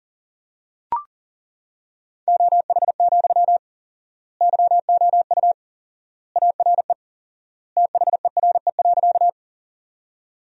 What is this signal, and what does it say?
Morse code, heard as a keyed tone, about 40 wpm, 700 Hz. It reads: OH, YOU ARE THERE.
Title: 4x Word-Spacing - Sentences from Top 500 Words - 40wpm